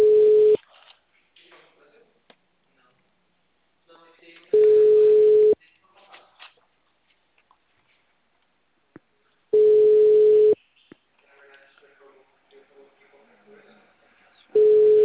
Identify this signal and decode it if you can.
Ringback tone (Europe, 425 Hz), with speech.